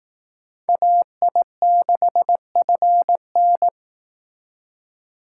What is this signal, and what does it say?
Morse code, heard as a keyed tone, about 18 wpm, 700 Hz. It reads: AI6FN
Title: Speed Racing - USA 2x2 Call Signs 15wpm